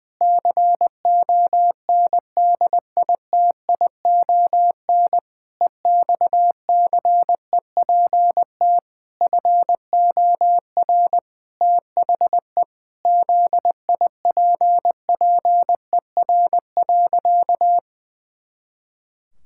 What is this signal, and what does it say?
Morse code, heard as a keyed tone, about 21 wpm, 700 Hz. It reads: CONDITION EXCEPT FOR THE ZIPPER.
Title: Pangram Challenge v2 - 20wpm